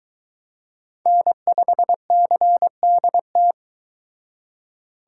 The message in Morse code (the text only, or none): N5CDT